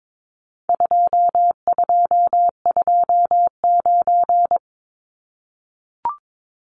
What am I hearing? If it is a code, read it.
Morse code, heard as a keyed tone, about 22 wpm, 700 Hz. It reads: 2229
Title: Numbers - 4-Digit 22wpm